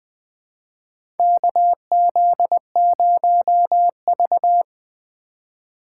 Morse code, heard as a keyed tone, about 20 wpm, 700 Hz. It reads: KZ0V
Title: USA Call Signs - 4 Characters - 4x Character Spacing - 20wpm